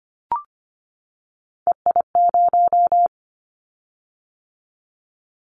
Morse code, heard as a keyed tone, about 25 wpm, 700 Hz. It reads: EI0
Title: Callsign Prefixes for Top 50 DXCC - 25wpm